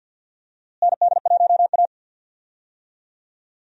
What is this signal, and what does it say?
Morse code, heard as a keyed tone, about 51 wpm, 700 Hz. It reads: ND1A
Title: USA Call Signs - 4 Characters - Repeated 3x - 50wpm